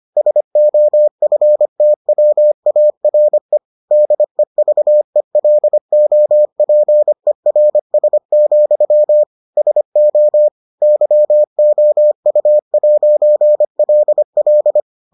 Morse code, heard as a keyed tone, about 25 wpm, 600 Hz. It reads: SOFTWARE DEVELOPERS, SO YOU'LL